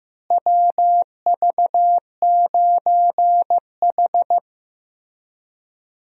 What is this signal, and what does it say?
Morse code, heard as a keyed tone, about 16 wpm, 700 Hz. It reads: WV9H
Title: USA Call Signs - 4 Characters - Repeated 3x - 15wpm